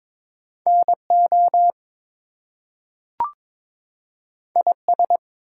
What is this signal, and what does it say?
Morse code, heard as a keyed tone, about 22 wpm, 700 Hz. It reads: NO IS
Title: Two Letter Words - 22wpm